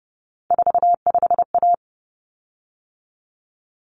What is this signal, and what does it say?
Morse code, heard as a keyed tone, about 30 wpm, 700 Hz. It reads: 45A
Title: Common Cut Numbers - 3-Digit 30wpm